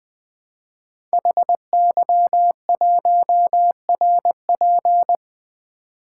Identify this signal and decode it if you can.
Morse code, heard as a keyed tone, about 20 wpm, 700 Hz. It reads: HY1RP
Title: Code Groups - 5 Letter:Numbers - 20wpm